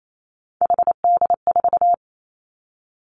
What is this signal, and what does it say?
Morse code, heard as a keyed tone, about 28 wpm, 700 Hz. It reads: HD4
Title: Three Challenging Letter-Number Combinations 28wpm